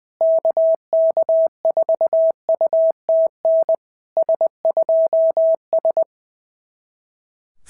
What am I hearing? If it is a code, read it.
Morse code, heard as a keyed tone, about 20 wpm, 650 Hz. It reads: KK4UTN S2S
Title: SOTA QSO Practice 20wpm